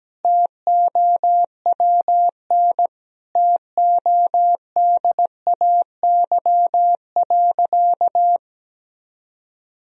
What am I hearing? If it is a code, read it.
Morse code, heard as a keyed tone, about 17 wpm, 700 Hz. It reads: TOWN TODAY.
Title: Sentences from Top 1000 Words - Review Part 3 - 17wpm